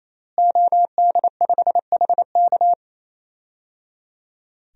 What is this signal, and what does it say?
Morse code, heard as a keyed tone, about 28 wpm, 700 Hz. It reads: OD5HK
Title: DXCC-100 Call Signs - 5 Characters - Repeated 3x - 28wpm